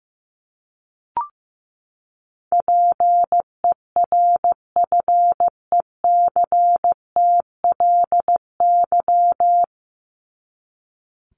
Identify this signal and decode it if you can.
Morse code, heard as a keyed tone, about 15 wpm, 700 Hz. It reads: PERFECTLY
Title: LY Suffix - Top 100 Words with Base Words 15wpm